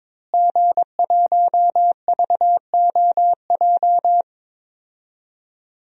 Morse code, heard as a keyed tone, about 22 wpm, 700 Hz. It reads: G1VOJ
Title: DXCC-100 Call Signs - 5 Characters - Repeated 3x - 22wpm